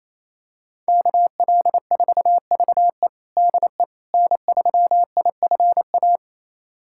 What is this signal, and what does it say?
Morse code, heard as a keyed tone, about 28 wpm, 700 Hz. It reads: KL4VE DE N3IFA
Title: USA Callsign DE Callsign - 5 Characters - 28wpm